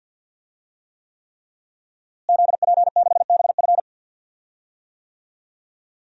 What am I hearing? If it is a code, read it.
Morse code, heard as a keyed tone, about 52 wpm, 700 Hz. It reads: ZP6BF
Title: DXCC-100 Call Signs - 5 Characters - Repeated 3x - 50wpm